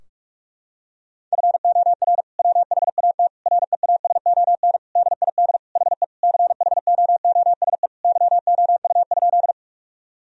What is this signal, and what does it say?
Morse code, heard as a keyed tone, about 42 wpm, 700 Hz. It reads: FOR WHAT REASON DID HE CHOOSE YOU?
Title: Sentences from Top 1000 Words - Review Part 4 - 45wpm